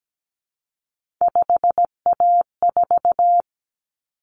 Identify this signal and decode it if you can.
Morse code, heard as a keyed tone, about 17 wpm, 700 Hz. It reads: 5A4
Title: Common Cut Numbers - 3-Digit 17wpm